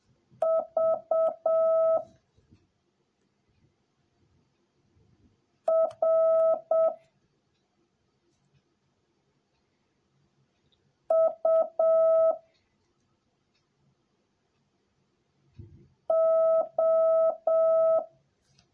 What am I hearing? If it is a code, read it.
Morse code, heard as a keyed tone, about 7 wpm, 650 Hz. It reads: V R U O